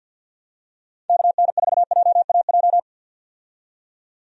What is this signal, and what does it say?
Morse code, heard as a keyed tone, about 49 wpm, 700 Hz. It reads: KN4JAJ
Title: USA Call Signs - 6 Characters - 50wpm